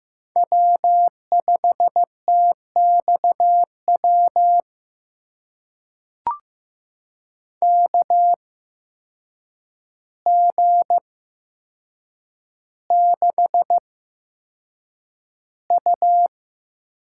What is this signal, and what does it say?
Morse code, heard as a keyed tone, about 15 wpm, spaced as for 7 wpm, 700 Hz. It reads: W5TXW KG6U
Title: USA Call Signs - 5 Characters - 8x Character Spacing - 15wpm